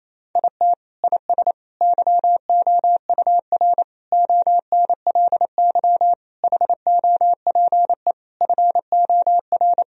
Morse code, heard as a keyed tone, about 28 wpm, 700 Hz. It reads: IT IS YOUR ONLY HOPE FOR
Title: Sentences from Top 1000 Words - Review Part 4 - 28wpm